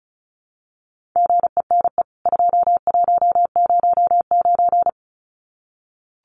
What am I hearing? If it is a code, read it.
Morse code, heard as a keyed tone, about 34 wpm, 700 Hz. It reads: GENE 2109
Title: CWT Exchange 35wpm